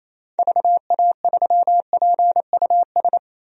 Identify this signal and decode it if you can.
Morse code, heard as a keyed tone, about 28 wpm, 700 Hz. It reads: VA3PUS
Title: DXCC-100 Call Signs - 6 Characters - Repeated 2x - 28wpm